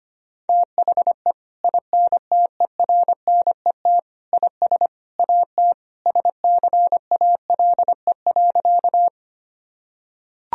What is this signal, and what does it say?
Morse code, heard as a keyed tone, about 25 wpm, 700 Hz. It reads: THE INTERNET IS AT SCALE.
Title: Sentences from Top 1000 Words - Review Part 4 - 25wpm